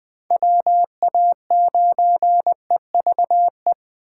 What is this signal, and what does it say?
Morse code, heard as a keyed tone, about 21 wpm, 700 Hz. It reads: WA9EVE
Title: Speed Racing - USA 2x3 Call Signs 17wpm